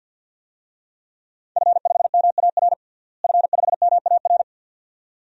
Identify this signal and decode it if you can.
Morse code, heard as a keyed tone, about 52 wpm, 700 Hz. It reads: F5MAR F5MAR
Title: DXCC-100 Call Signs - 5 Characters - Repeated 2x - 50wpm